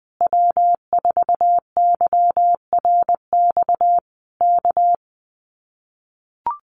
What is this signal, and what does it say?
Morse code, heard as a keyed tone, about 19 wpm, 700 Hz. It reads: W4YRX K
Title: Common QSO Phrases Encore 20wpm